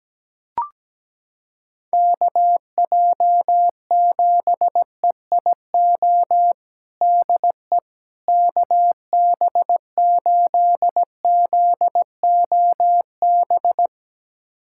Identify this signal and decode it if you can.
Morse code, heard as a keyed tone, about 17 wpm, 700 Hz. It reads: KJ7EIO DE KB8ZOB
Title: USA Callsign DE Callsign 17wpm